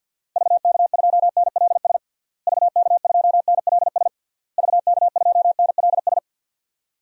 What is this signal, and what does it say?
Morse code, heard as a keyed tone, about 51 wpm, 700 Hz. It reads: VK2NLS VK2NLS VK2NLS
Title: DXCC-100 Call Signs - 6 Characters - Repeated 3x - 50wpm